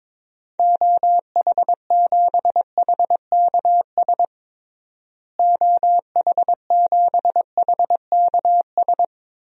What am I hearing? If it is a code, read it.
Morse code, heard as a keyed tone, about 22 wpm, 700 Hz. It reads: OH7HKS OH7HKS
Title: DXCC-100 Call Signs - 6 Characters - Repeated 3x - 22wpm